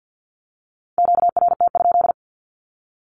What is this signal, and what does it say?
Morse code, heard as a keyed tone, about 50 wpm, 700 Hz. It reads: QRT?
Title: Q-Codes with Spoken Meaning - 50wpm